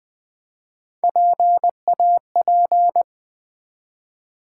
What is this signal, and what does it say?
Morse code, heard as a keyed tone, about 20 wpm, 700 Hz. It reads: PAP
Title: Build A Word - 1 Letter At A Time - 5-Letter Words - 20wpm